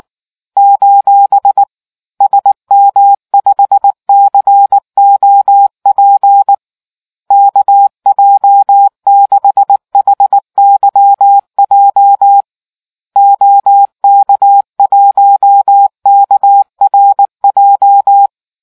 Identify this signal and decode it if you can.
Morse code, heard as a keyed tone, about 19 wpm, 800 Hz. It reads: : SM5COP KJ6HYJ OK1KRJ